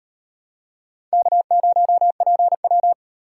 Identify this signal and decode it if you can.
Morse code, heard as a keyed tone, about 38 wpm, 700 Hz. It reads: K0PW